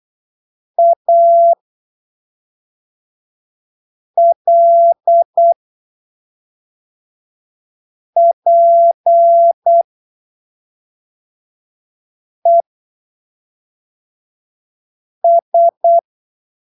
Morse code, heard as a keyed tone, about 8 wpm, spaced as for 3 wpm, 680 Hz. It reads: ALPES
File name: arquivos / texto / texto 26 08ppm